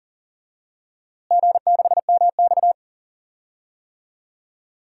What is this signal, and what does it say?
Morse code, heard as a keyed tone, about 40 wpm, 700 Hz. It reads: G6MX
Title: DXCC-100 Call Signs - 4 Characters - Repeated 3x - 40wpm